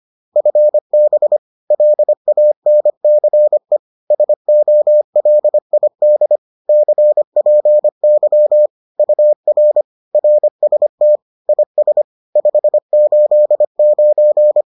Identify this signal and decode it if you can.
Morse code, heard as a keyed tone, about 25 wpm, 600 Hz. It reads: FB LANCE SOLID CPY UR RST IS 589